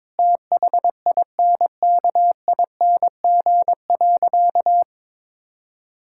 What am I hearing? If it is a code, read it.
Morse code, heard as a keyed tone, about 22 wpm, 700 Hz. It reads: THINKING.
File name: Difficult Sentences with Hints - 22wpm